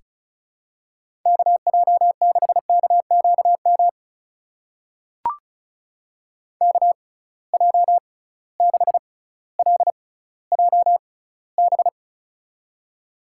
Morse code, heard as a keyed tone, about 36 wpm, spaced as for 18 wpm, 700 Hz. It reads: KJ6KQM KJ6LJB
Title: USA Call Signs - 6 Characters - 6x Character Spacing - 35wpm